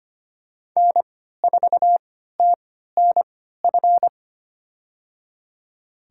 Morse code, heard as a keyed tone, about 25 wpm, spaced as for 14 wpm, 700 Hz. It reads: N4TNF
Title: USA Call Signs - 5 Characters - 3x Character Spacing - 25wpm